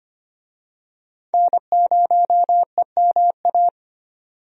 Morse code, heard as a keyed tone, about 25 wpm, 700 Hz. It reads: N0EMA
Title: USA Call Signs - 5 Characters - 25wpm